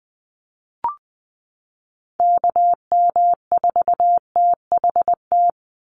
Morse code, heard as a keyed tone, about 20 wpm, 700 Hz. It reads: KM4THT
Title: USA Call Signs 20wpm